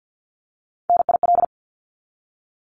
Morse code, heard as a keyed tone, about 48 wpm, 700 Hz. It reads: NIL
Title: Abbreviations 50wpm